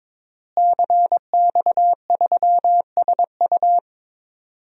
Morse code, heard as a keyed tone, about 22 wpm, 700 Hz. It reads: CX3SU
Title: DXCC-100 Call Signs - 5 Characters - Repeated 3x - 22wpm